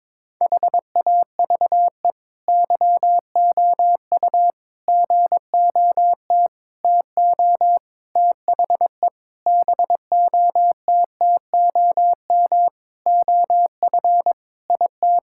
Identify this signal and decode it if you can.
Morse code, heard as a keyed tone, about 22 wpm, 700 Hz. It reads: HAVE YOU GOT TO THE BOTTOM OF IT
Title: Sentences from Top 1000 Words - Review Part 6 - 22wpm